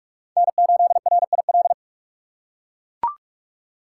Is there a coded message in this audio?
Yes: Morse code from 0.4 to 1.7 s, heard as a keyed tone, about 41 wpm, 700 Hz. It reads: N8RIL